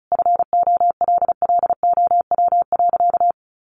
Morse code, heard as a keyed tone, about 34 wpm, 700 Hz. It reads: FOLLOW.
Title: Sentences from Top 1500 Words 35wpm